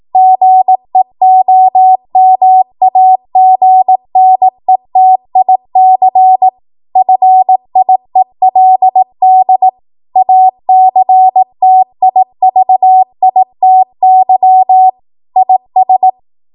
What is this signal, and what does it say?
Morse code, heard as a keyed tone, about 18 wpm, 750 Hz. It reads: GEOMAGNETIC FIELD ACTIVITY IS
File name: Morse / 18WPM